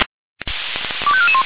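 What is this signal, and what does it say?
Five-tone selective call (ZVEI), heard as quick runs of five tones, tones of about 70 ms. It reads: 25441
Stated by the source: Five-tone ZVEI selective call, received by SDR, fire brigade Orlamünde.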